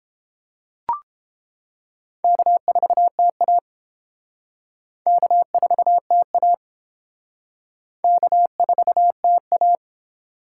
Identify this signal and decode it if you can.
Morse code, heard as a keyed tone, about 31 wpm, 700 Hz. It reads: K4TA K4TA K4TA